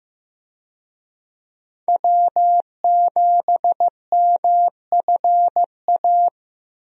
Morse code, heard as a keyed tone, about 15 wpm, 700 Hz. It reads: W7MFA
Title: USA Call Signs - 5 Characters - Repeated 2x - 15wpm